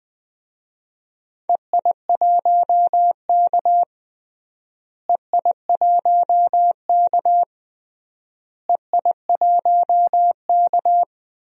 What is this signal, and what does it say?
Morse code, heard as a keyed tone, about 21 wpm, 700 Hz. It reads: EI1K EI1K EI1K
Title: DXCC-100 Call Signs - 4 Characters - Repeated 3x - 20wpm